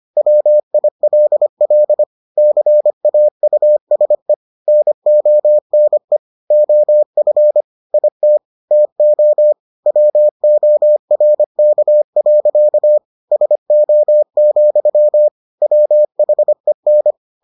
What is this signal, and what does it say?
Morse code, heard as a keyed tone, about 25 wpm, 600 Hz. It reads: WILL CAUSE NONE OF IT TO WORK. SO, WHEN